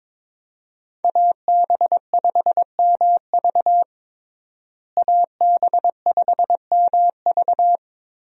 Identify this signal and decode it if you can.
Morse code, heard as a keyed tone, about 22 wpm, 700 Hz. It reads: AB5MV AB5MV
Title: USA Call Signs - 5 Characters - Repeated 3x - 22wpm